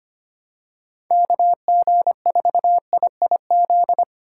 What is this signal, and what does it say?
Morse code, heard as a keyed tone, about 25 wpm, 700 Hz. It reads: KG4IIZ
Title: USA Call Signs - 6 Characters - 4x Character Spacing - 25wpm